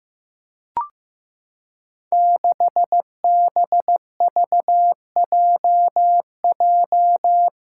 Morse code, heard as a keyed tone, about 15 wpm, 700 Hz. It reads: 6BVJJ
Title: Five Challenging Letter-Number Combinations 15wpm